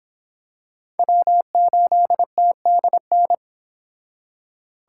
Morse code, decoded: W8TDN

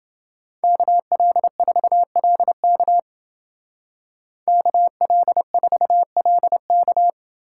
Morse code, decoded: KL4LK KL4LK

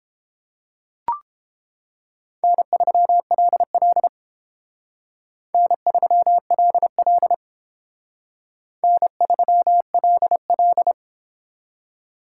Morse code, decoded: N3LL N3LL N3LL

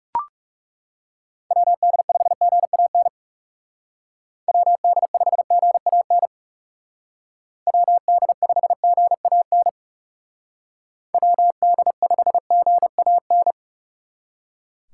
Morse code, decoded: WD5GAN WD5GAN WD5GAN WD5GAN

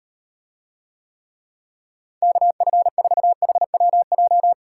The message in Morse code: KF4HWJ

